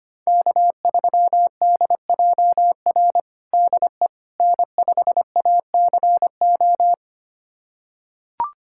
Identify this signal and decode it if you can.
Morse code, heard as a keyed tone, about 25 wpm, 700 Hz. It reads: K3DJR DE N5ACO
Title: USA Callsign DE Callsign - 5 Characters - 25wpm